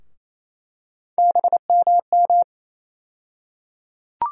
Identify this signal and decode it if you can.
Morse code, heard as a keyed tone, about 28 wpm, 700 Hz. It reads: BMM
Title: USA Call Sign Postfixes - 3 Letters 28wpm